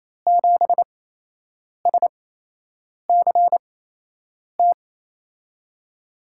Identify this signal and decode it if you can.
Morse code, heard as a keyed tone, about 28 wpm, spaced as for 8 wpm, 700 Hz. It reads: 7SCT
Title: USA Call Signs - 8x Character Spacing - 28wpm